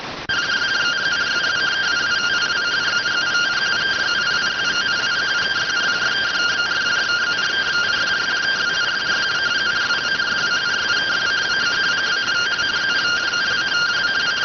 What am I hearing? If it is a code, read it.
Two-tone teleprinter signal, about 170 Hz shift.